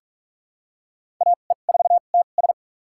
Morse code, heard as a keyed tone, about 43 wpm, spaced as for 31 wpm, 700 Hz. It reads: AE4TS
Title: USA Call Signs - 5 Characters - 2x Character Spacing - 45wpm